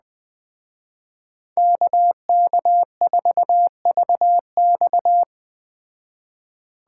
Morse code, heard as a keyed tone, about 20 wpm, 700 Hz. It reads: KK4VX
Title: Speed Racing - USA 2x2 Call Signs 17wpm